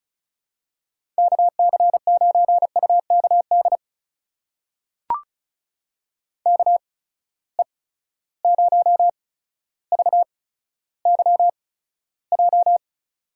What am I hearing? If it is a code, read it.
Morse code, heard as a keyed tone, about 36 wpm, spaced as for 15 wpm, 700 Hz. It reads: KC9UKD KE0VYJ
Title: USA Call Signs - 6 Characters - 8x Character Spacing - 35wpm